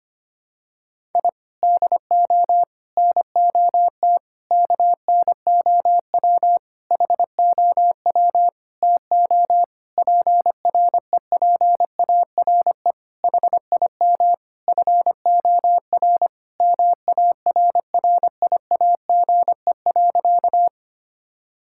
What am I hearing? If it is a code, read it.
Morse code, heard as a keyed tone, about 25 wpm, 700 Hz. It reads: I DO NOT KNOW HOW TO PREPARE HIM FOR MARRIAGE.
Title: Sentences from Top 1000 Words - Review Part 6 - 25wpm